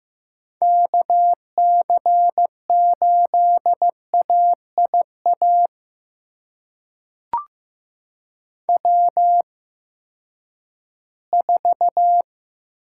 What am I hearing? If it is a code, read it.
Morse code, heard as a keyed tone, about 15 wpm, spaced as for 10 wpm, 700 Hz. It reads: KC8AIA W4
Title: USA Call Signs - 8x Character Spacing - 15wpm